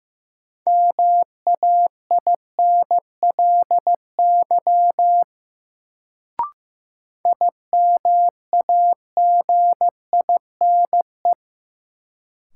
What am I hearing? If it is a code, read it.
Morse code, heard as a keyed tone, about 16 wpm, 700 Hz. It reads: MAINLY IMAGINE
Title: Sentences from Top 1400 Words - New Word Drill Alt - 15wpm